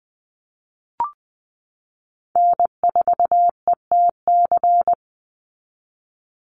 Morse code, heard as a keyed tone, about 20 wpm, 700 Hz. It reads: N4ETC
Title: Call Signs - 20wpm